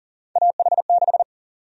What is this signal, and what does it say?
Morse code, heard as a keyed tone, about 40 wpm, 700 Hz. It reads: AH6
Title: USA Call Sign Prefixes - 40wpm